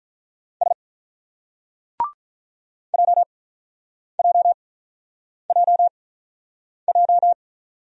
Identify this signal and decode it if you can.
Morse code, heard as a keyed tone, about 43 wpm, spaced as for 8 wpm, 700 Hz. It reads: S JJJJ
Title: Speed-Racing Single Letters 35wpm